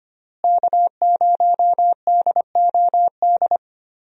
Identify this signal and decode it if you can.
Morse code, heard as a keyed tone, about 25 wpm, 700 Hz. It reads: K0DOD
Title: USA Call Signs - 5 Characters - Repeated 3x - 25wpm